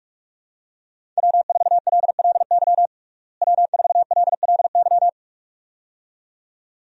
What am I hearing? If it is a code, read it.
Morse code, heard as a keyed tone, about 46 wpm, 700 Hz. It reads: W4LLY W4LLY
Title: USA Call Signs - 5 Characters - Repeated 2x - 45wpm